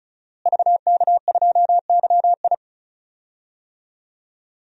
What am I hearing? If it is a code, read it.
Morse code, heard as a keyed tone, about 34 wpm, 700 Hz. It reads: VK2YI
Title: DXCC-100 Call Signs - 5 Characters - Repeated 2x - 35wpm